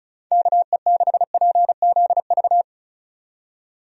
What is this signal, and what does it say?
Morse code, heard as a keyed tone, about 36 wpm, 700 Hz. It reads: KE6PZV
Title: USA Call Signs - 6 Characters - 35wpm